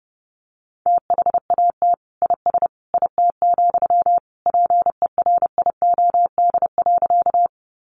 Morse code, heard as a keyed tone, about 30 wpm, 700 Hz. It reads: THAT IS IT, PERIOD.
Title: Sentences from Top 800 Words 30wpm